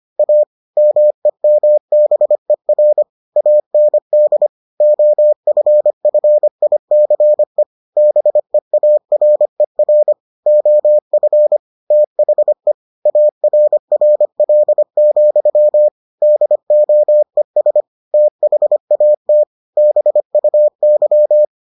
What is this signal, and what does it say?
Morse code, heard as a keyed tone, about 25 wpm, 600 Hz. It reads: A MEMBER AND OFFICE BEARER OF THE ARRL, DOES THAT BUY